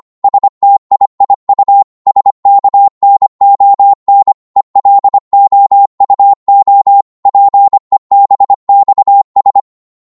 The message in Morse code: STIIU SKNON ELOUO PEBXS